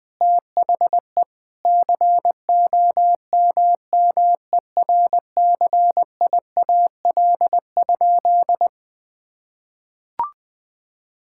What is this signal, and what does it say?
Morse code, heard as a keyed tone, about 21 wpm, 700 Hz. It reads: THE COMMERCIAL?
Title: Sentences from Top 1000 Words - Review Part 5 - 20wpm